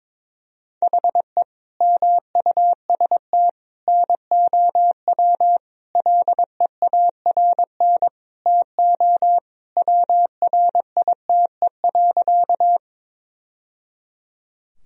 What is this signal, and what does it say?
Morse code, heard as a keyed tone, about 22 wpm, 700 Hz. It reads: HE MUST NOW LEARN TO WRITE.
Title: Sentences from Top 1000 Words - Review Part 1 - 22wpm